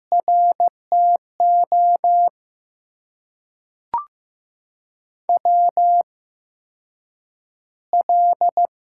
Morse code, heard as a keyed tone, about 15 wpm, spaced as for 5 wpm, 700 Hz. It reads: RTOWL
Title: USA Call Signs - 5 Characters - 8x Character Spacing - 15wpm